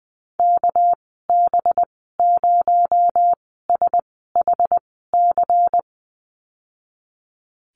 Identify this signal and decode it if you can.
Morse code, heard as a keyed tone, about 20 wpm, spaced as for 14 wpm, 700 Hz. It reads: KB0SHC
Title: 238 Lesson 38g - Letter q - Call Signs 20wpm